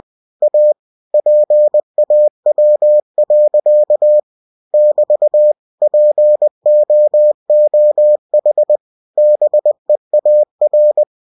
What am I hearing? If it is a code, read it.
Morse code, heard as a keyed tone, about 20 wpm, 600 Hz. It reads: A PAW. <BT> POOH BEAR
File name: cw-books / winnie-the-pooh / winnie-20wpm-page-0017